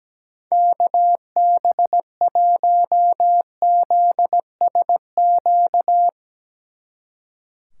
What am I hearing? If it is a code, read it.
Morse code, heard as a keyed tone, about 17 wpm, 700 Hz. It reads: KB1ZSQ
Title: Speed Racing - USA 2x3 Call Signs 17wpm